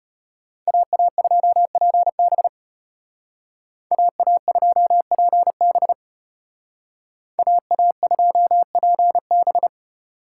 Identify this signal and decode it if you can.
Morse code, heard as a keyed tone, about 30 wpm, 700 Hz. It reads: AA2PB AA2PB AA2PB
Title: Speed Racing - USA 2x2 Call Signs 25wpm